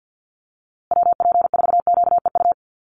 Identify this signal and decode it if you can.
Morse code, heard as a keyed tone, about 50 wpm, 700 Hz. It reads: WP4QEU
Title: USA Call Signs Repeated 3x 50wpm